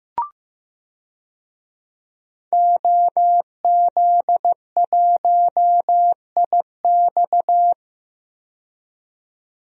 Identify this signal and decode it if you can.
Morse code, heard as a keyed tone, about 15 wpm, 700 Hz. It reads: OZ1IX
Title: DXCC-100 Call Signs - 5 Characters - 15wpm